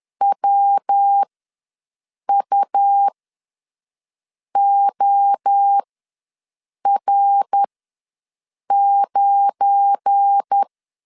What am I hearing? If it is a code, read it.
Morse code, heard as a keyed tone, about 10 wpm, spaced as for 6 wpm, 800 Hz. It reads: WUOR9
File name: LEZIONI / 10 SIGNS / 10C DOT ws 10wpm 4213